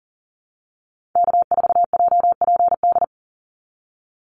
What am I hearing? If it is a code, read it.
Morse code, heard as a keyed tone, about 40 wpm, 700 Hz. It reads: K4JPD
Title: Call Signs Repeated 2x - 40wpm